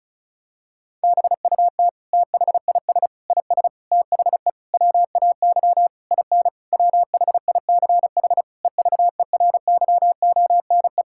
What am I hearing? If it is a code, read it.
Morse code, heard as a keyed tone, about 36 wpm, 700 Hz. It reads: BUT THIS IS THE WAY IN WHICH EVERYONE
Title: Sentences from Top 1000 Words - Review Part 5 - 35wpm